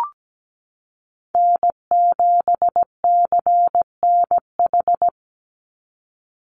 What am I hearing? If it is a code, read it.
Morse code, heard as a keyed tone, about 17 wpm, 700 Hz. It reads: N7CNH
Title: NA SOTA Chaser Callsigns 17wpm